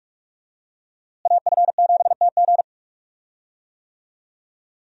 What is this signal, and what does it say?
Morse code, heard as a keyed tone, about 43 wpm, 700 Hz. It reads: AF7TG
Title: USA Call Signs - 5 Characters - Repeated 3x - 45wpm